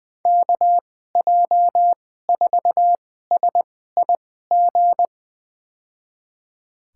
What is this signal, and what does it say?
Morse code, heard as a keyed tone, about 20 wpm, spaced as for 14 wpm, 700 Hz. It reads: KJ4SIG
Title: USA Call Signs - 6 Characters - 2x Character Spacing - 20wpm